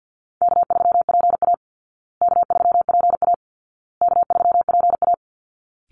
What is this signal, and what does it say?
Morse code, heard as a keyed tone, about 50 wpm, 700 Hz. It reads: K3PA K3PA K3PA
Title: Easy Call Signs Repeated 3x - 50wpm